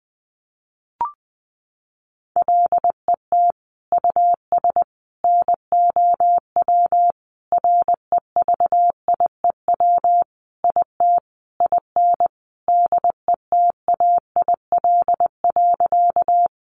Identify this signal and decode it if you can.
Morse code, heard as a keyed tone, about 20 wpm, 700 Hz. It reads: LET US NOW REVIEW IT IN DETAIL.
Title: Sentences from Top 1500 Words 20wpm